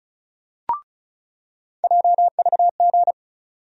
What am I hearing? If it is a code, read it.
Morse code, heard as a keyed tone, about 34 wpm, 700 Hz. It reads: JVG
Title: USA Call Sign Postfixes - 3 Letters 35wpm